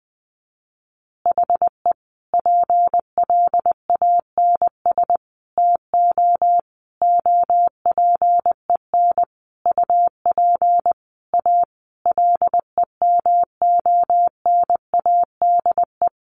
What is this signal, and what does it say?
Morse code, heard as a keyed tone, about 20 wpm, 700 Hz. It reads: HE PLANS TO OPEN UP A LEMONADE
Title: Additional Easy Phrases 20wpm